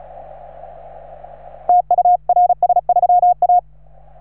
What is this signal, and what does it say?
Morse code, heard as a keyed tone, about 36 wpm, 720 Hz. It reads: TU RS3A